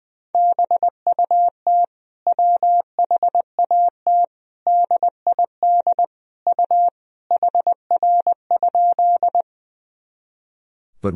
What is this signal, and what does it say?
Morse code, heard as a keyed tone, about 21 wpm, 700 Hz. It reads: BUT WHAT DID U HR?
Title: Sentences from Top 300 Words and Abbreviations 20wpm